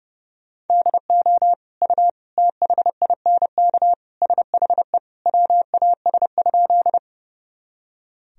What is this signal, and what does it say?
Morse code, heard as a keyed tone, about 30 wpm, 700 Hz. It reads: DO U THINK SHE WAS?
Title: Sentences from Top 200 Words and Abbreviations 30wpm